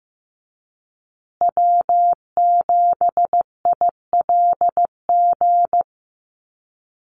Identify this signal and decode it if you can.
Morse code, heard as a keyed tone, about 15 wpm, 700 Hz. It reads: W7ILG
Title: USA Call Signs Repeated 3x 15wpm